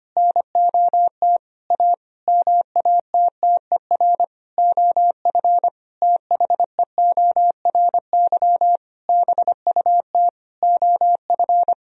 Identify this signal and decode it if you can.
Morse code, heard as a keyed tone, about 25 wpm, 700 Hz. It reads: NOT A MATTER OF THEORY BUT OF
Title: Sentences from Top 1000 Words - Review Part 4 - 25wpm